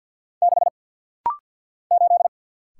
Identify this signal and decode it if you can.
Morse code, heard as a keyed tone, about 50 wpm, 700 Hz. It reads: 6 8
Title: Single Letter-Number - Rapid-Fire 50wpm